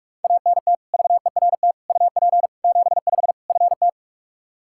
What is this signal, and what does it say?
Morse code, heard as a keyed tone, about 45 wpm, 700 Hz. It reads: ANT VERT UP 75 FT
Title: K1USN SST QSO Practice 45wpm